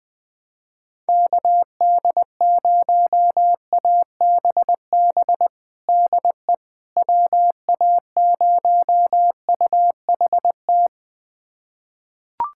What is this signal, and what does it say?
Morse code, heard as a keyed tone, about 20 wpm, 700 Hz. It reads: KD0ABB DE WA0UHT
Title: USA Callsign DE Callsign - 6 Characters - 20wpm